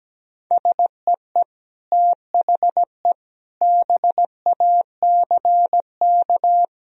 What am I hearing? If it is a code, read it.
Morse code, heard as a keyed tone, about 17 wpm, 700 Hz. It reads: SEE THE BACK
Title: Sentences from Top 1000 Words - Review Part 1 - 17wpm